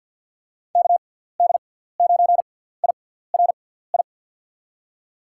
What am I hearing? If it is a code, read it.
Morse code, heard as a keyed tone, about 50 wpm, spaced as for 17 wpm, 700 Hz. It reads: KD9IRI
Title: USA Call Signs - 6 Characters - 6x Character Spacing - 50wpm